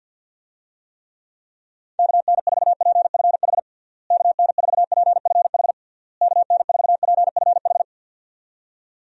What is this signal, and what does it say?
Morse code, heard as a keyed tone, about 51 wpm, 700 Hz. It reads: KN4PFH KN4PFH KN4PFH
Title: USA Call Signs - 6 Characters - Repeated 3x - 50wpm